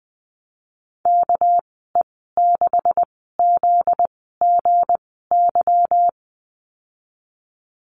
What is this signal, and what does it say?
Morse code, heard as a keyed tone, about 20 wpm, spaced as for 15 wpm, 700 Hz. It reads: KE6ZGY
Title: 245 Lesson 39g - Letter z - Call Signs 20wpm